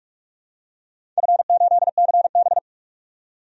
Morse code, heard as a keyed tone, about 47 wpm, 700 Hz. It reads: F8CB
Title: DXCC-100 Call Signs - 4 Characters - 45wpm